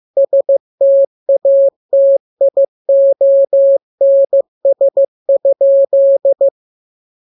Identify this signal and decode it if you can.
Morse code, heard as a keyed tone, about 15 wpm, 550 Hz. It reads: STATIONS?
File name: jokes / joke 20 15 0000